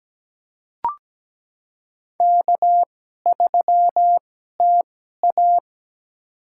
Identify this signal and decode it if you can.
Morse code, heard as a keyed tone, about 17 wpm, spaced as for 12 wpm, 700 Hz. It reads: K3TA
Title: USA Call Signs - 4 Characters - 2x Character Spacing - 17wpm